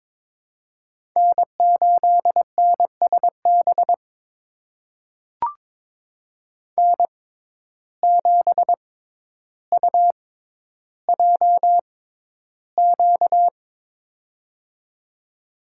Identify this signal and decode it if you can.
Morse code, heard as a keyed tone, about 22 wpm, spaced as for 11 wpm, 700 Hz. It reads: N8NSB N7UJQ
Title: USA Call Signs - 5 Characters - 6x Character Spacing - 22wpm